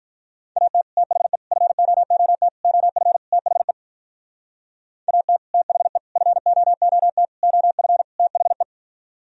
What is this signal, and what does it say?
Morse code, heard as a keyed tone, about 43 wpm, 700 Hz. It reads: AT THE FOOT OF THE AT THE FOOT OF THE
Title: Speed-Racing - Sets of 5 Words 35wpm